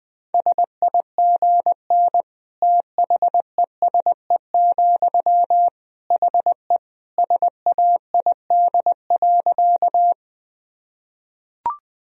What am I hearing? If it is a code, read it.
Morse code, heard as a keyed tone, about 21 wpm, 700 Hz. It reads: SIGN THESE, HE SAID.
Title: Sentences from Top 1000 Words - Review Part 1 - 20wpm